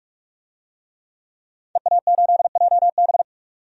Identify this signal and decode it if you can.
Morse code, heard as a keyed tone, about 42 wpm, 700 Hz. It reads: EA8JB
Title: DXCC-100 Call Signs - 5 Characters - 45wpm